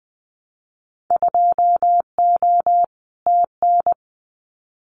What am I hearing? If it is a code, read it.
Morse code, heard as a keyed tone, about 20 wpm, 700 Hz. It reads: 2O TN
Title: Winter Field Day Exchanges 20wpm